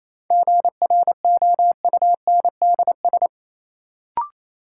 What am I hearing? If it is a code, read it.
Morse code, heard as a keyed tone, about 27 wpm, 700 Hz. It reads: GROUNDS
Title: Sentences from Top 1300 Words - New Word Drill Alt - 28wpm